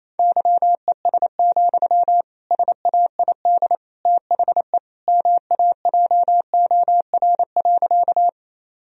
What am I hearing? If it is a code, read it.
Morse code, heard as a keyed tone, about 28 wpm, 700 Hz. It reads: YES, SAID THE MAJOR.
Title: Sentences from Top 1000 Words - Review Part 4 - 28wpm